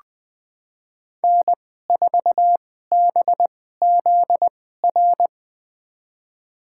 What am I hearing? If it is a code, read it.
Morse code, heard as a keyed tone, about 21 wpm, spaced as for 15 wpm, 700 Hz. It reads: N4BZR
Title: USA Call Signs - 2x Character Spacing - 20wpm